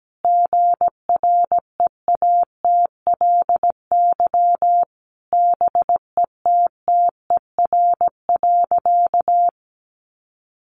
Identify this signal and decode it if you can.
Morse code, heard as a keyed tone, about 17 wpm, 700 Hz. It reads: GREATLY BETTER.
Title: Sentences from Top 1400 Words 17wpm